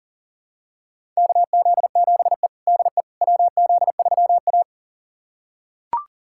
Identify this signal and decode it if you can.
Morse code, heard as a keyed tone, about 40 wpm, 700 Hz. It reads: KZ7E DE WZ3A
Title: USA Callsign DE Callsign - 4 Characters - 40wpm